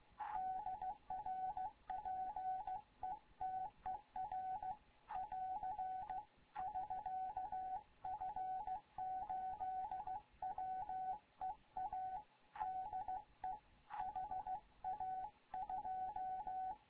Morse code, heard as a keyed tone, about 16 wpm, 750 Hz, with radio noise. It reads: DR PETER <AR> <SK> F8WEA DE HA2
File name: Morze-HA2MN / hangp / 2 11 GYAKORLAT